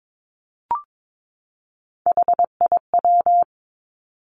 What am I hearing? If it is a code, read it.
Morse code, heard as a keyed tone, about 22 wpm, 700 Hz. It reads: HIW
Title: Three Challenging Letter-Number Combinations 22wpm